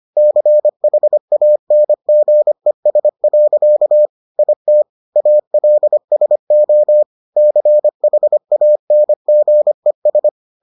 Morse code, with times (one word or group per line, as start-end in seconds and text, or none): 0.2-4.1 CHANGES.
4.4-4.8 IT
5.2-7.0 ALSO
7.4-10.3 CHANGES